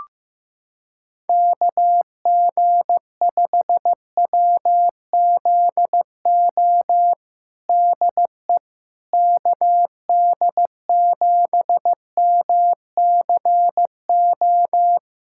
Morse code, heard as a keyed tone, about 15 wpm, 700 Hz. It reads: KG5WZO DE KD7MCO
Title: USA Callsign DE Callsign 15wpm